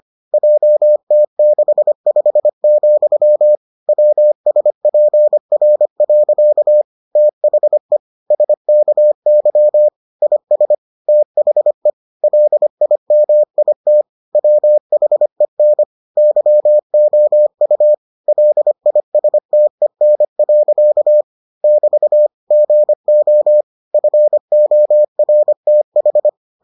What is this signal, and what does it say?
Morse code, heard as a keyed tone, about 25 wpm, 600 Hz. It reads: JT65, WSPR. THE SKY IS THE LIMIT WHEN YOU LISTEN. <BT> GO FORTH